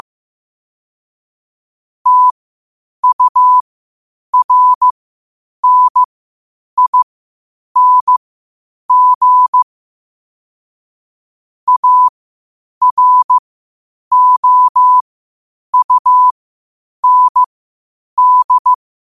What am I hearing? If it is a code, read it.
Morse code, heard as a keyed tone, about 15 wpm, spaced as for 8 wpm, 1000 Hz. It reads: TURNING AROUND
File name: qotdfiles / 8wpm